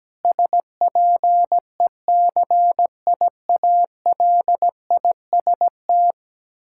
Morse code, heard as a keyed tone, about 17 wpm, 700 Hz. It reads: SPECIALIST
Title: Practice QSO - Exchange 3 - More Info - 17wpm